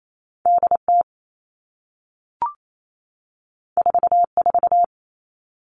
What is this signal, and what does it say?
Morse code, heard as a keyed tone, about 28 wpm, 700 Hz. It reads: DT 44
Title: All Cut Numbers - 2-Digit 28wpm